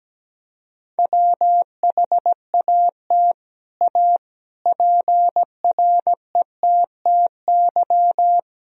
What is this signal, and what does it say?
Morse code, heard as a keyed tone, about 17 wpm, 700 Hz. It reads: WHAT A PRETTY
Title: Sentences from Top 1000 Words - Review Part 2 - 17wpm